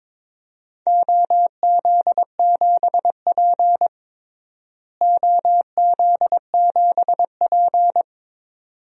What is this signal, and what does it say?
Morse code, heard as a keyed tone, about 21 wpm, 700 Hz. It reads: OZ7P OZ7P
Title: DXCC-100 Call Signs - 4 Characters - Repeated 2x - 22wpm